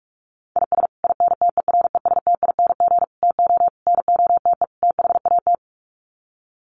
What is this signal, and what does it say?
Morse code, heard as a keyed tone, about 44 wpm, 690 Hz. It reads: IS INTERESTING TO NOTE THAT
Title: Sets of 5 Words Encore2 45wpm